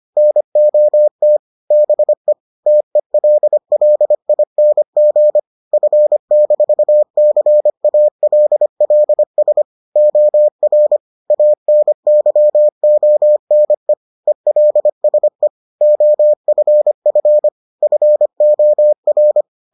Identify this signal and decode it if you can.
Morse code, heard as a keyed tone, about 25 wpm, 600 Hz. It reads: NOT BE TELLING F-CALLS OR ANYONE ELSE OFF FOR